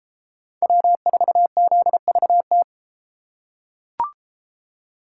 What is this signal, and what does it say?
Morse code, heard as a keyed tone, about 31 wpm, 700 Hz. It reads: W4ZVT